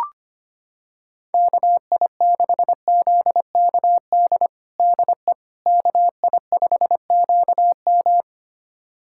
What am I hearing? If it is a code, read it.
Morse code, heard as a keyed tone, about 25 wpm, 700 Hz. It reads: KI6ZKD DE KI5QM
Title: USA Callsign DE Callsign 25wpm